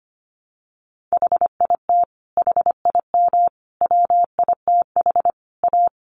Morse code, heard as a keyed tone, about 25 wpm, 700 Hz. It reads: HIT HIM WITH A
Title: Sentences from Top 1200 Words 25wpm